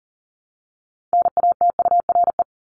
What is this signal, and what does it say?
Morse code, heard as a keyed tone, about 40 wpm, 700 Hz. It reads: NATURE
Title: Top 501-750 Words 40wpm